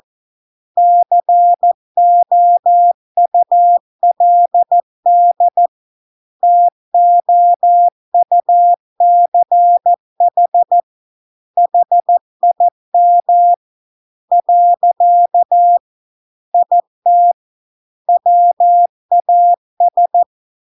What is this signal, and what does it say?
Morse code, heard as a keyed tone, about 14 wpm, 700 Hz. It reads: COULD TOUCH HIM . IT WAS